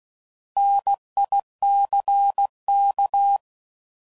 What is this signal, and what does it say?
Morse code, heard as a keyed tone, about 16 wpm, 800 Hz. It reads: NICK